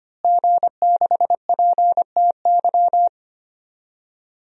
Morse code, heard as a keyed tone, about 25 wpm, 700 Hz. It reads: G6PTY
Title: DXCC-100 Call Signs - 5 Characters - Repeated 2x - 25wpm